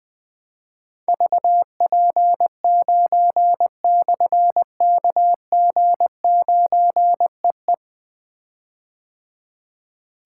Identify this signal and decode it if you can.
Morse code, heard as a keyed tone, about 20 wpm, 700 Hz. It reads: VP9/KG9EE
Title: Difficult Call Signs 20wpm